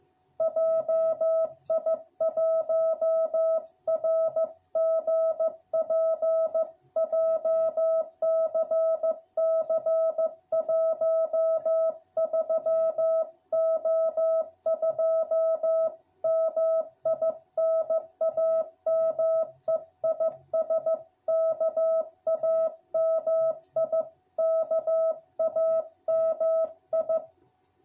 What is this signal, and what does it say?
Morse code, heard as a keyed tone, about 15 wpm, 660 Hz. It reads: JI1RGP JCC 13O2 MI NAME IS KAMI KAMI